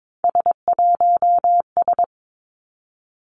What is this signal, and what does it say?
Morse code, heard as a keyed tone, about 22 wpm, 700 Hz. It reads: S1S
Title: Three Challenging Letter-Number Combinations 22wpm